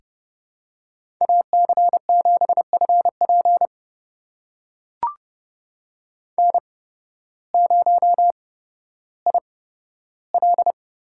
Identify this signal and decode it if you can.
Morse code, heard as a keyed tone, about 30 wpm, spaced as for 14 wpm, 700 Hz. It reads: AC7FP N0IL
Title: USA Call Signs - 5 Characters - 8x Character Spacing - 30wpm